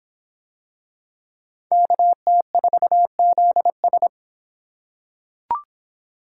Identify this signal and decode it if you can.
Morse code, heard as a keyed tone, about 26 wpm, 700 Hz. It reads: KT4ZS